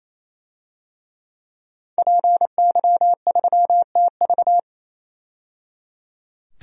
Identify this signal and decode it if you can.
Morse code, heard as a keyed tone, about 28 wpm, 700 Hz. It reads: PY3TV
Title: DXCC-100 Call Signs - 5 Characters - 28wpm